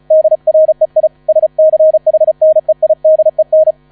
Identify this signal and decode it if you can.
Morse code, heard as a keyed tone, about 35 wpm, 630 Hz. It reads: DREI SCHNEIDEN
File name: 35wpm